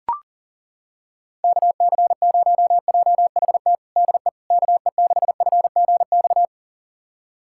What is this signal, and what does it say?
Morse code, heard as a keyed tone, about 40 wpm, 700 Hz. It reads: KC0JHT DE KE6FGX
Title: USA Callsign DE Callsign - 6 Characters - 40wpm